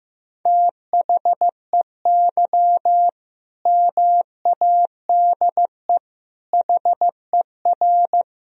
Morse code, heard as a keyed tone, about 15 wpm, 700 Hz. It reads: THEY MADE HER
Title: Sentences from Top 1000 Words - Review Part 2 - 15wpm